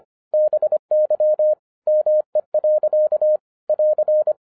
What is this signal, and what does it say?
Morse code, heard as a keyed tone, about 25 wpm, 610 Hz. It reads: BY ME. <AR>
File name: 2011-07-mp3 / n59 BACK FROM A BIG ACCIDENT 25wpm